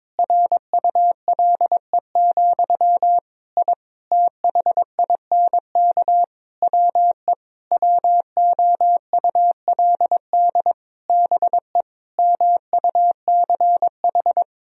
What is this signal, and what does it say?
Morse code, heard as a keyed tone, about 22 wpm, 700 Hz. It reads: RULE, I THINK WE WOULD BE MUCH
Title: Sentences from Top 1000 Words - Review Part 3 - 22wpm